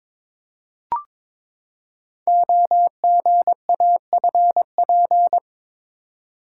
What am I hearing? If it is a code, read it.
Morse code, heard as a keyed tone, about 22 wpm, 700 Hz. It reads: OGAFP
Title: Code Groups - 5 Letters - 22wpm